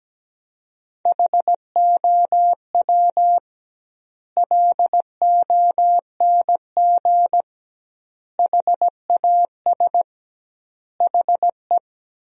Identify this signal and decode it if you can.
Morse code, heard as a keyed tone, about 17 wpm, 700 Hz. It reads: HOW LONG HAS HE
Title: 2x Word-Spacing - Sentences from Top 100 Words - 17wpm